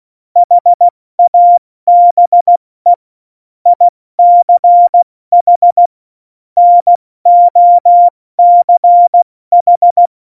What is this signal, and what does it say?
Morse code, heard as a keyed tone, about 16 wpm, 700 Hz. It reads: HABE ICH NOCH
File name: Morsekurs / Witze / Witz 05. Dezember 2017, 70 F80